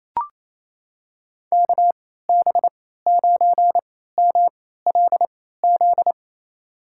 Morse code, heard as a keyed tone, about 28 wpm, spaced as for 16 wpm, 700 Hz. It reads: KB9MLZ
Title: USA Call Signs - 6 Characters - 3x Character Spacing - 28wpm